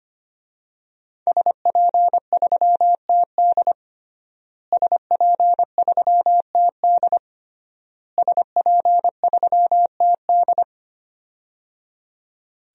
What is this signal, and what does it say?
Morse code, heard as a keyed tone, about 25 wpm, 700 Hz. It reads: SP3TD SP3TD SP3TD
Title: DXCC-100 Call Signs - 5 Characters - Repeated 3x - 25wpm